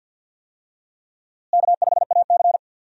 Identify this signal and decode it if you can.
Morse code, heard as a keyed tone, about 52 wpm, 700 Hz. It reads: K5AC
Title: USA Call Signs - 4 Characters - Repeated 2x - 50wpm